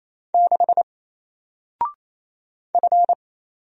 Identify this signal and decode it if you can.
Morse code, heard as a keyed tone, about 27 wpm, 700 Hz. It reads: 6 F
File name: Single-Letter-Number-28wpm